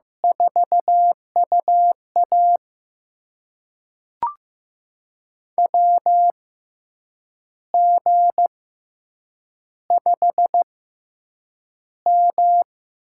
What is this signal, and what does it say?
Morse code, heard as a keyed tone, about 15 wpm, spaced as for 7 wpm, 700 Hz. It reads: 4UA WG5M
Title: USA Call Signs - 4 Characters - 6x Character Spacing - 15wpm